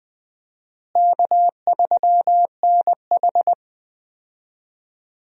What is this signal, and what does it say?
Morse code, heard as a keyed tone, about 20 wpm, 700 Hz. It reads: K3NH
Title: USA Call Signs - 4 Characters - Repeated 3x - 20wpm